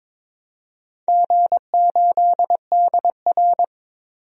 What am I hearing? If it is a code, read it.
Morse code, heard as a keyed tone, about 23 wpm, 700 Hz. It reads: G8DR
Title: DXCC-100 Call Signs - 4 Characters - 22wpm